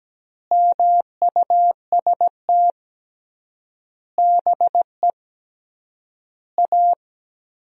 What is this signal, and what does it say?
Morse code, heard as a keyed tone, about 17 wpm, 700 Hz. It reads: MUST BE A
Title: 3x Word-Spacing - Sentences from Top 500 Words - 17wpm